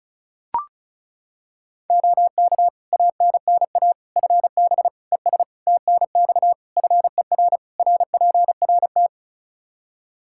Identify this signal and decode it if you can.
Morse code, heard as a keyed tone, about 34 wpm, 700 Hz. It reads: OK ANNA FB ES TNX FER RPRT
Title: Practice QSO - Exchange 2 - Info - 35wpm